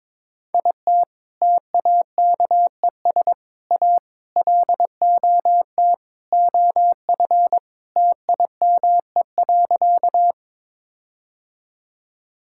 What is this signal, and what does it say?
Morse code, heard as a keyed tone, about 22 wpm, 700 Hz. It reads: IT TAKES A LOT OF TIME.
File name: practice / 22wpm / Sentences with 4-Word Phrases - 22wpm